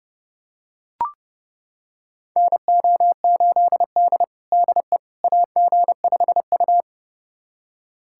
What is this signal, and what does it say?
Morse code, heard as a keyed tone, about 30 wpm, 700 Hz. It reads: NO8D DE AG5U
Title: USA Callsign DE Callsign - 4 Characters - 30wpm